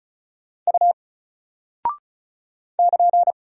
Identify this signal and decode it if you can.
Morse code, heard as a keyed tone, about 36 wpm, 700 Hz. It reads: U <KN>
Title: Single Character (Letter:Number:Punctuation:Prosign) 35wpm